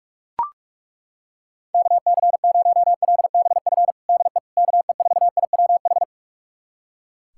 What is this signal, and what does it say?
Morse code, heard as a keyed tone, about 41 wpm, 690 Hz. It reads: KC0LBF DE KE4IWH
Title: USA Callsign DE Callsign - 6 Characters - 45wpm